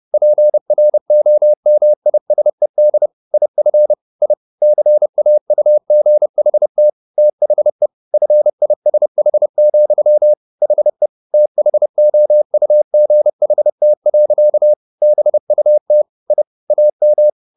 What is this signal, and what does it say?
Morse code, heard as a keyed tone, about 30 wpm, 600 Hz. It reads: PROMISED IF I CAUGHT THE FISH, HE THOUGHT. BUT I AM